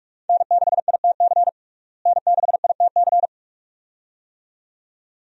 Morse code, heard as a keyed tone, about 46 wpm, 700 Hz. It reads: N6ITC N6ITC
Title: USA Call Signs - 5 Characters - Repeated 2x - 45wpm